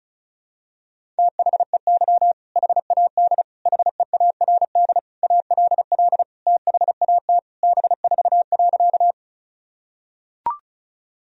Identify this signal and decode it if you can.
Morse code, heard as a keyed tone, about 34 wpm, 700 Hz. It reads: THEY HAD HEARD ALL THAT B4.
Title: Sentences from Top 400 Words and Abbreviations 35wpm